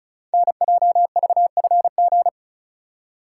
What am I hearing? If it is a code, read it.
Morse code, heard as a keyed tone, about 36 wpm, 700 Hz. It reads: NJVFG
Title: Code Groups - 5 Letters - 35wpm